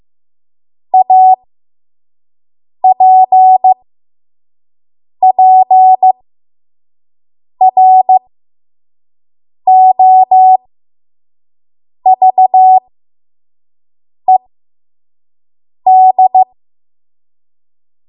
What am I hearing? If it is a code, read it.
Morse code, heard as a keyed tone, about 15 wpm, spaced as for 5 wpm, 750 Hz. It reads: APPROVED
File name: Morse / 05WPM